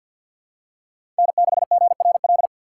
Morse code, heard as a keyed tone, about 52 wpm, 700 Hz. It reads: N6GRL
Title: USA Call Signs - 5 Characters - 3x Character Spacing - 50wpm